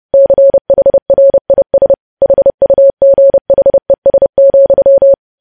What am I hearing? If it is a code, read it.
Morse code, heard as a keyed tone, about 30 wpm, 550 Hz. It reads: CHRIS HUGHES,
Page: news headlines in morse code at 30 wpm